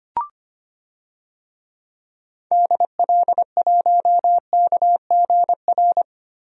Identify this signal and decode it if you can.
Morse code, heard as a keyed tone, about 25 wpm, 700 Hz. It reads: DL1KGR
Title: DXCC-100 Call Signs - 6 Characters - Repeated 3x - 25wpm